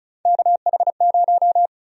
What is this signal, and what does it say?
Morse code, heard as a keyed tone, about 36 wpm, 700 Hz. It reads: KH0
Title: USA Call Sign Prefixes - 35wpm